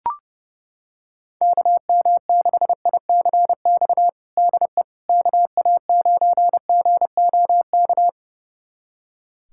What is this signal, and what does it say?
Morse code, heard as a keyed tone, about 32 wpm, 700 Hz. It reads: KM6ICX DE KA9GOK
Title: USA Callsign DE Callsign - 6 Characters - 30wpm